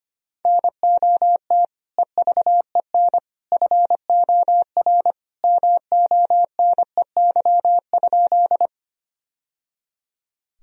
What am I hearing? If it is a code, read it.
Morse code, heard as a keyed tone, about 25 wpm, 700 Hz. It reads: NOT EVEN FOR MONEY?
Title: Sentences from Top 1000 Words - Review Part 1 - 25wpm